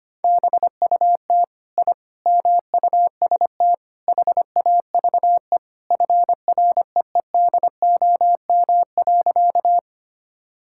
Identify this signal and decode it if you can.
Morse code, heard as a keyed tone, about 25 wpm, 700 Hz. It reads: BUT I MUST HAVE FREEDOM.
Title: Sentences from Top 1000 Words - Review Part 5 - 25wpm